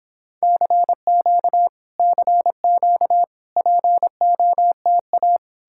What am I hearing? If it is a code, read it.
Morse code, heard as a keyed tone, about 31 wpm, 700 Hz. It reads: CQ CQ POTA